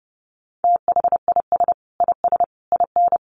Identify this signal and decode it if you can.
Morse code, heard as a keyed tone, about 30 wpm, 700 Hz. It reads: THIS IS IN
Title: Sentences from Top 1500 Words 30wpm